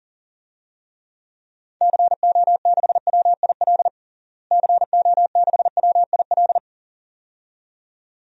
Morse code, heard as a keyed tone, about 40 wpm, 700 Hz. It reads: CO6WIL CO6WIL
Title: DXCC-100 Call Signs - 6 Characters - Repeated 2x - 40wpm